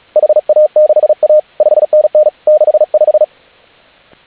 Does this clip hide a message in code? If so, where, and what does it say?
Yes: Morse code from 0.2 to 3.2 s, heard as a keyed tone, about 36 wpm, 610 Hz. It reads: HA6A 5NN 65